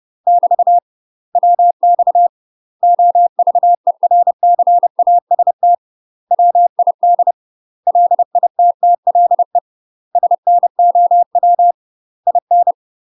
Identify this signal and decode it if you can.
Morse code, heard as a keyed tone, about 30 wpm, 700 Hz. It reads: <BT> WX OVERCAST WID LITTLE SNOW IN
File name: UK MASTER QSOs TRAINING / 30WPM / UK MASTER QSOs 0026